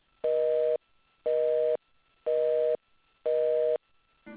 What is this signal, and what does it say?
Busy tone (North America), 480 + 620 Hz.
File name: things / rolm busy moh no supe